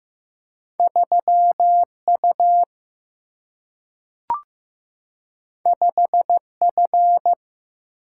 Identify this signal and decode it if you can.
Morse code, heard as a keyed tone, about 15 wpm, 700 Hz. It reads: 3U 5F
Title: Single-Letter Single-Number 15wpm